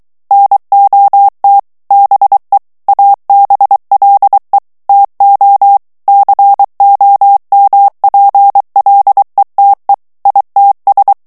Morse code, heard as a keyed tone, about 23 wpm, 800 Hz. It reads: NOT BE ABLE TO COMPLETE ITS
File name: ANGLAIS texte clair 3 vitesse 22 mots minute